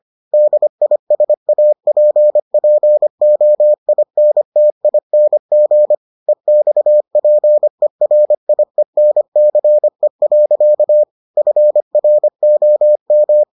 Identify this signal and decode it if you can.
Morse code, heard as a keyed tone, about 25 wpm, 600 Hz. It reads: DISAPPOINTING EXPERIENCE. FROM